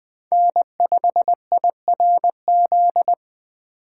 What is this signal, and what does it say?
Morse code, heard as a keyed tone, about 20 wpm, 700 Hz. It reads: N5IRZ
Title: USA Call Signs - 5 Characters - 2x Character Spacing - 20wpm